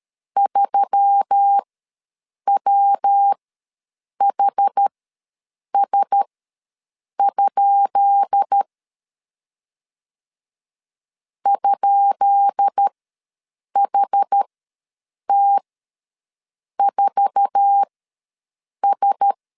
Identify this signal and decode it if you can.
Morse code, heard as a keyed tone, about 13 wpm, spaced as for 7 wpm, 800 Hz. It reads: 3WHS? ?HT4S
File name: LEZIONI / 10 SIGNS / 10B INT ws 12wpm 3527